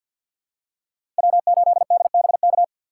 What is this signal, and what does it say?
Morse code, heard as a keyed tone, about 52 wpm, 690 Hz. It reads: W8DBK
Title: USA Call Signs - 5 Characters - 2x Character Spacing - 50wpm